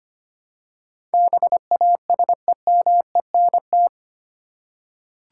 Morse code, heard as a keyed tone, about 25 wpm, 690 Hz. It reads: BASEMENT
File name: 8 Letter Words - First 3 Letters Given - 25wpm